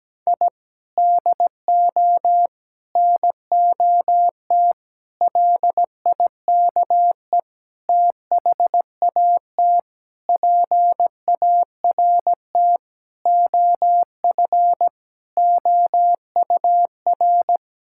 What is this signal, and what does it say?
Morse code, heard as a keyed tone, about 17 wpm, 700 Hz. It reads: I DO NOT LIKE THAT PART OF OUR
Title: Sentences from Top 1000 Words - Review Part 6 - 17wpm